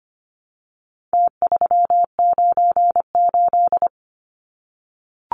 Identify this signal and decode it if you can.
Morse code, heard as a keyed tone, about 25 wpm, 700 Hz. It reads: T398
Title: Numbers - Contest Style - 4-Digit 25wpm